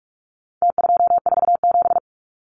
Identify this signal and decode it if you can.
Morse code, heard as a keyed tone, about 44 wpm, 700 Hz. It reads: T247
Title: Numbers - Contest Style - 4-Digit 45wpm